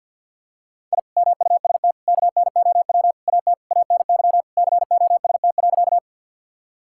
Morse code, heard as a keyed tone, about 51 wpm, 700 Hz. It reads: I MUST KNOW AT ANY COST.
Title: Sentences from Top 1000 Words - Review Part 5 - 50wpm